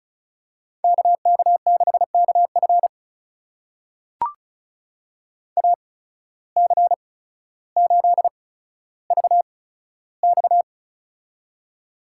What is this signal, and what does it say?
Morse code, heard as a keyed tone, about 36 wpm, spaced as for 15 wpm, 700 Hz. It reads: KK6KF AC8VX
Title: USA Call Signs - 5 Characters - 8x Character Spacing - 35wpm